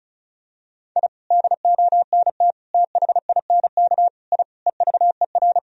Morse code, heard as a keyed tone, about 34 wpm, 700 Hz. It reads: I DONT THINK I EVER
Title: Sentences from Top 1000 Words - Review Part 4 - 35wpm